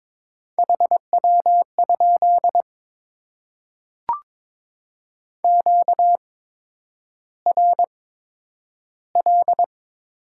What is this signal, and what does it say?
Morse code, heard as a keyed tone, about 22 wpm, spaced as for 9 wpm, 700 Hz. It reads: HW? QRL
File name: Top 20 QSO Elements - 8x Character Spacing - 22wpm